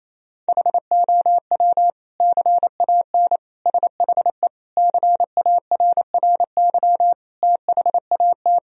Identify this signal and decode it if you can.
Morse code, heard as a keyed tone, about 28 wpm, 700 Hz. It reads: HOW CAN SHE CARRY THAT
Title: Sentences from Top 1000 Words - Review Part 3 - 28wpm